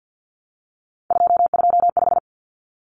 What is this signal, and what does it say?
Morse code, heard as a keyed tone, about 50 wpm, 700 Hz. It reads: 225
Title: Common Cut Numbers - 3-Digit 50wpm